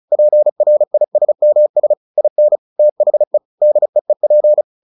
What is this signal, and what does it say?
Morse code, heard as a keyed tone, about 34 wpm, 600 Hz. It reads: PRISMS IN THE DEEP
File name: the old man and the sea / the old man and the sea 1 35wpm